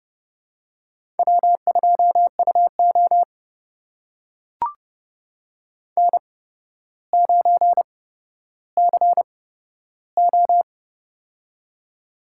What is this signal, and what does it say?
Morse code, heard as a keyed tone, about 31 wpm, spaced as for 13 wpm, 700 Hz. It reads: W2UO N9CO
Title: USA Call Signs - 4 Characters - 8x Character Spacing - 30wpm